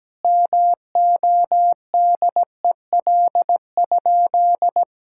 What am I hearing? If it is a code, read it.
Morse code, heard as a keyed tone, about 17 wpm, 700 Hz. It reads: MODEL?
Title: Sentences from Top 1000 Words - Review Part 5 - 17wpm